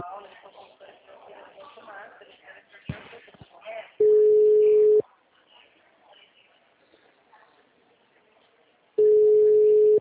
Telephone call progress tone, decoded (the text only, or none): ringback tone